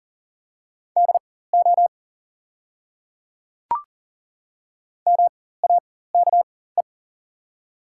Morse code, heard as a keyed tone, about 39 wpm, spaced as for 19 wpm, 700 Hz. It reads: DO MAKE
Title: 4x Character Spacing - Top 100 Words - 40wpm